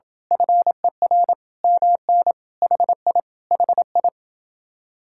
Morse code, heard as a keyed tone, about 28 wpm, 700 Hz. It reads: FER MN HI HI